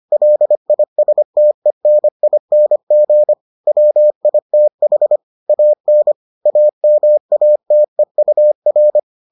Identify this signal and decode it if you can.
Morse code, heard as a keyed tone, about 25 wpm, 600 Hz. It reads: LISTENING WITH AN AMATEUR